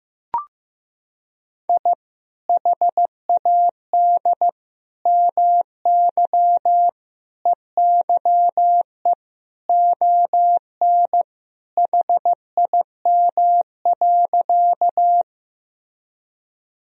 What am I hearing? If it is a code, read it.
Morse code, heard as a keyed tone, about 15 wpm, 700 Hz. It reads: I HAD MY EYE ON HIM.
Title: Sentences from Top 1000 Words - Review Part 1 - 15wpm